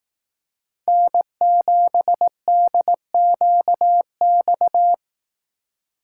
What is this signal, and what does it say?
Morse code, heard as a keyed tone, about 18 wpm, 700 Hz. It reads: N7DQX
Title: Speed Racing - USA 1x3 Call Signs 15wpm